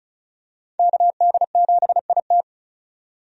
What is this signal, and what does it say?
Morse code, heard as a keyed tone, about 36 wpm, 700 Hz. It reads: KD7IT
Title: USA Call Signs - 5 Characters - 35wpm